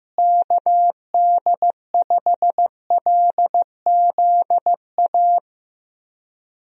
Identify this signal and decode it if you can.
Morse code, heard as a keyed tone, about 15 wpm, 700 Hz. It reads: KD5LZA
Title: USA Call Signs - 6 Characters - Repeated 3x - 15wpm